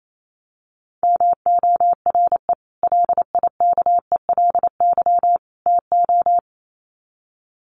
Morse code, heard as a keyed tone, about 28 wpm, 700 Hz. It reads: MORE LIKELY TO
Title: Sets of 3 Words 28wpm